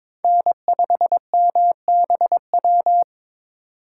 Morse code, heard as a keyed tone, about 22 wpm, 700 Hz. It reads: N5MBW
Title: USA Call Signs - 5 Characters - 3x Character Spacing - 22wpm